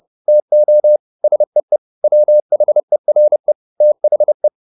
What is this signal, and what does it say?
Morse code, heard as a keyed tone, about 30 wpm, 600 Hz. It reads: TO SEE WHERE THE
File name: the old man and the sea / the old man and the sea 1 30wpm